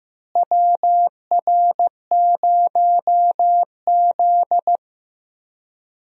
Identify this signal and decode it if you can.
Morse code, heard as a keyed tone, about 15 wpm, 700 Hz. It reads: WR0Z
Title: USA Call Signs - 4 Characters - Repeated 2x - 15wpm